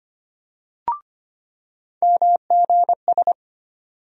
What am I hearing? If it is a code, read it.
Morse code, heard as a keyed tone, about 25 wpm, 700 Hz. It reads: MGS
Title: USA Call Sign Postfixes - 3 Letters 25wpm